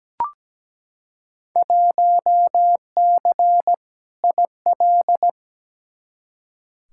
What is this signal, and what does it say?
Morse code, heard as a keyed tone, about 17 wpm, 700 Hz. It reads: 1C IL
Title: Field Day Exchanges - 17wpm